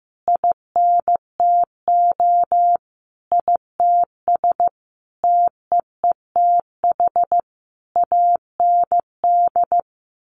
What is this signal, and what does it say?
Morse code, heard as a keyed tone, about 15 wpm, 700 Hz. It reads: INTO ITS TEETH AND
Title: A Sound Of Thunder 15wpm